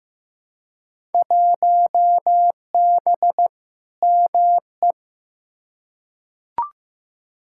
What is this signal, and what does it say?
Morse code, heard as a keyed tone, about 15 wpm, 700 Hz. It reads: 1B ME
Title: Field Day Exchanges - 15wpm